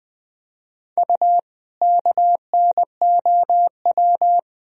Morse code, heard as a keyed tone, about 20 wpm, 700 Hz. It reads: U KNOW
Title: Sentences from Top 100 Words and Abbreviations 20wpm